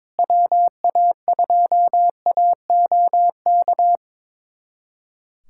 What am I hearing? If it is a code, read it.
Morse code, heard as a keyed tone, about 22 wpm, 700 Hz. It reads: WA2AOK
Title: USA Callsign DE Callsign - 6 Characters - 22wpm